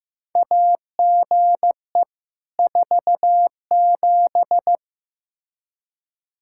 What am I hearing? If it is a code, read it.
Morse code, heard as a keyed tone, about 15 wpm, 700 Hz. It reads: AGE 47
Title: Practice QSO - Exchange 3 - More Info - 15wpm